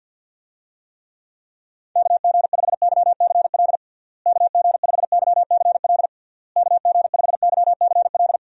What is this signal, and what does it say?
Morse code, heard as a keyed tone, about 51 wpm, 700 Hz. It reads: KG5YCL KG5YCL KG5YCL
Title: USA Call Signs - 6 Characters - Repeated 3x - 50wpm